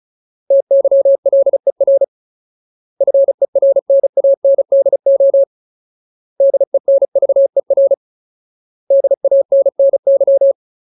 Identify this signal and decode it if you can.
Morse code, heard as a keyed tone, about 34 wpm, 550 Hz. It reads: TYLER FERNANDO DENVER DANNY
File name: Ultimate Word List / 35WPM Ultimate